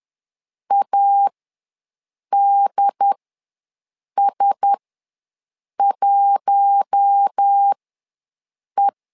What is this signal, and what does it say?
Morse code, heard as a keyed tone, about 10 wpm, spaced as for 6 wpm, 800 Hz. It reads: ADS1E
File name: LEZIONI / 4 Q1S ws 10wpm 2208